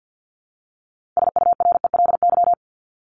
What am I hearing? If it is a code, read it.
Morse code, heard as a keyed tone, about 50 wpm, 700 Hz. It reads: SURELY
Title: LY Suffix - Top 100 Words with Base Words 50wpm